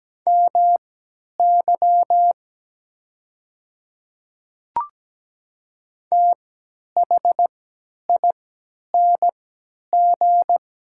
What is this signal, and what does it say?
Morse code, heard as a keyed tone, about 17 wpm, spaced as for 10 wpm, 700 Hz. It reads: MY THING
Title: 3x Character Spacing - Top 100 Words - 17wpm